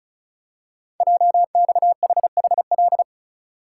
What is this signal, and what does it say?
Morse code, heard as a keyed tone, about 36 wpm, 700 Hz. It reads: JXHHL
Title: Code Groups - 5 Letters - 35wpm